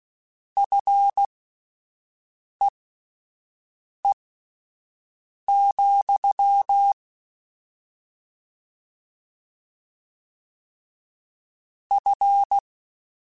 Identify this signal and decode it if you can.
Morse code, heard as a keyed tone, about 16 wpm, spaced as for 5 wpm, 800 Hz. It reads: FEE, F
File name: Combinations / Combinations-mp3s / Combinations@5